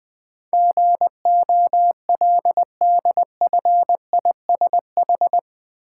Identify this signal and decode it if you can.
Morse code, heard as a keyed tone, about 21 wpm, 700 Hz. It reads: GOLDFISH
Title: Easy Words with Hints - 20wpm